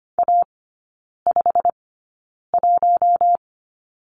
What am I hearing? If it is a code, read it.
Morse code, heard as a keyed tone, about 25 wpm, spaced as for 9 wpm, 690 Hz. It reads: A51
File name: Lesson 15e - Number 5 - Three Characters 25wpm